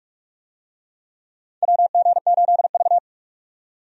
Morse code, heard as a keyed tone, about 46 wpm, 700 Hz. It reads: WG8V
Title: USA Call Signs - 4 Characters - 45wpm